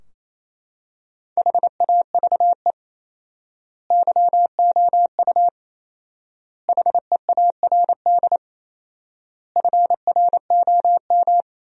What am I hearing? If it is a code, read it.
Morse code, heard as a keyed tone, about 27 wpm, 700 Hz. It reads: HAVE YOU HEARD FROM
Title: 4x Word-Spacing - Sentences from Top 400 Words - 28wpm